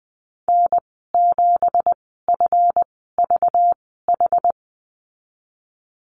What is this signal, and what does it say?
Morse code, heard as a keyed tone, about 20 wpm, spaced as for 15 wpm, 700 Hz. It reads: N7FVH
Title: 193 Lesson 31g - Letter v - Call Signs 20wpm